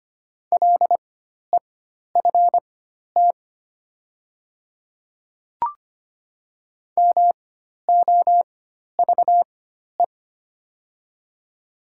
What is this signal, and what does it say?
Morse code, heard as a keyed tone, about 25 wpm, spaced as for 12 wpm, 690 Hz. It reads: LEFT MOVE
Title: 4x Character Spacing - Top 500 Words - 25wpm